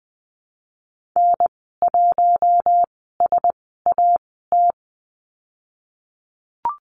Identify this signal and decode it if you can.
Morse code, heard as a keyed tone, about 20 wpm, spaced as for 15 wpm, 700 Hz. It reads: N1SAT
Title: 055 Lesson 10g - Letter r - Call Signs 20wpm